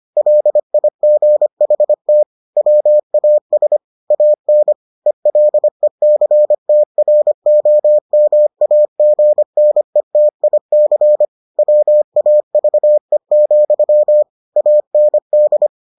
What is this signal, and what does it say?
Morse code, heard as a keyed tone, about 25 wpm, 600 Hz. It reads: LIGHT WAS AN ELECTROMAGNETIC WAVE, AND